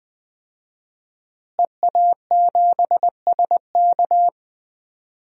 Morse code, heard as a keyed tone, about 20 wpm, 700 Hz. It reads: EA7SK
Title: DXCC-100 Call Signs - 5 Characters - Repeated 3x - 20wpm